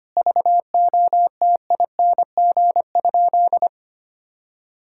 Morse code, heard as a keyed tone, about 25 wpm, 700 Hz. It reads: VOTING?
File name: practice / 25wpm / Sentences with 9-Word Phrases - 25wpm